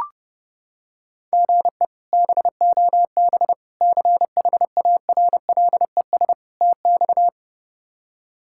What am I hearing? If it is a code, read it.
Morse code, heard as a keyed tone, about 31 wpm, 700 Hz. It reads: GE BOB CHARLES TX
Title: K1USN SST QSO Practice - 30wpm